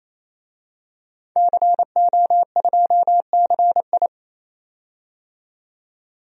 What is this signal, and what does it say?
Morse code, heard as a keyed tone, about 28 wpm, 690 Hz. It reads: CO2CI
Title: DXCC-100 Call Signs - 5 Characters - 28wpm